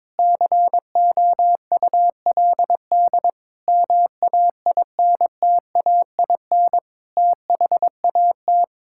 Morse code, heard as a keyed tone, about 22 wpm, 700 Hz. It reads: COULD MAINTAIN THAT
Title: Sentences from Top 1000 Words - Review Part 5 - 22wpm